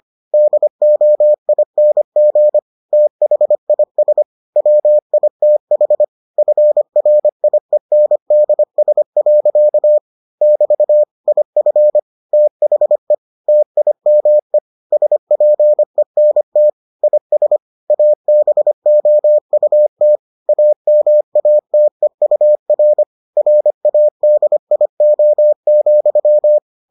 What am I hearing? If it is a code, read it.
Morse code, heard as a keyed tone, about 25 wpm, 600 Hz. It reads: DOING THIS WITH FRIENDS. <BT> IF THE TIME SPENT IS ABOUT AMATEUR RADIO,